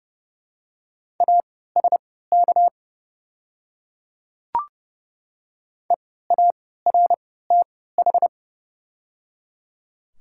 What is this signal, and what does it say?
Morse code, heard as a keyed tone, about 31 wpm, spaced as for 17 wpm, 700 Hz. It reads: ASK EARTH
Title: 3x Character Spacing - Top 200 Words - 30wpm